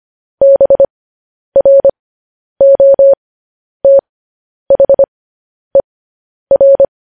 Morse code, heard as a keyed tone, about 25 wpm, spaced as for 10 wpm, 550 Hz. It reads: BROTHER
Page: news headlines in morse code at 10 wpm